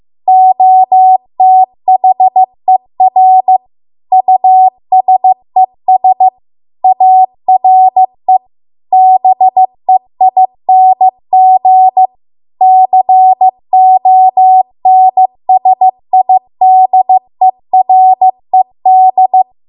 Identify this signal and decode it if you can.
Morse code, heard as a keyed tone, about 15 wpm, 750 Hz. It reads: OTHER USES ARE BEING CONSIDERED